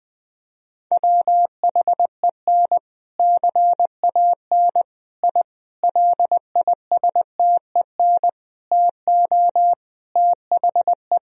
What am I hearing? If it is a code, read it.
Morse code, heard as a keyed tone, about 20 wpm, 700 Hz. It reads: WHEN CAN I LISTEN TO THE
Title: Sentences from Top 1000 Words - Review Part 3 - 20wpm